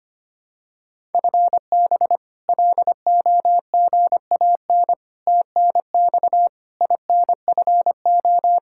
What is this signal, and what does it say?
Morse code, heard as a keyed tone, about 25 wpm, 700 Hz. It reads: FB LOGAN TNX INFO
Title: K1USN SST QSO Practice 25wpm